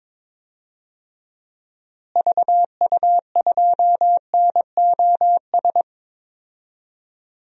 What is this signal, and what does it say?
Morse code, heard as a keyed tone, about 22 wpm, 700 Hz. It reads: VU2NOS
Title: DXCC-100 Call Signs - 6 Characters - Repeated 2x - 22wpm